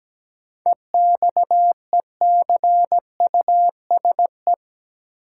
Morse code, heard as a keyed tone, about 17 wpm, 700 Hz. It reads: EXECUSE
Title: Sentences from Top 700 Words - New Word Drill - 17wpm